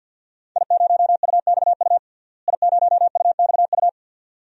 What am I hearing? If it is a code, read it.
Morse code, heard as a keyed tone, about 52 wpm, 700 Hz. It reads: I0UXU I0UXU
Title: DXCC-100 Call Signs - 5 Characters - Repeated 2x - 50wpm